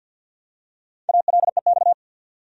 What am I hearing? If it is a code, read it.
Morse code, heard as a keyed tone, about 51 wpm, 700 Hz. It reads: ALEX
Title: USA Common Names - 50wpm